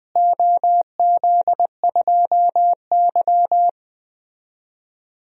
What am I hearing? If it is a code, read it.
Morse code, heard as a keyed tone, about 20 wpm, 700 Hz. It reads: OZ2Y
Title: DXCC-100 Call Signs - 4 Characters - Repeated 3x - 20wpm